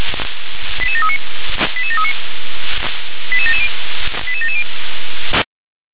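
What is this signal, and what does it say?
Five-tone selective call (ZVEI), heard as quick runs of five tones, tones of about 70 ms. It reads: 88739 88739 88770 88770
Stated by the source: ZVEI five-tone selective call received on an SDR.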